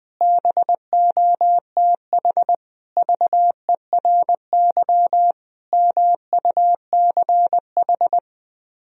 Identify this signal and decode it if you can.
Morse code, heard as a keyed tone, about 20 wpm, 700 Hz. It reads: BOTH VERY MUCH
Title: Speed-Racing - Sets of 5 Words 20wpm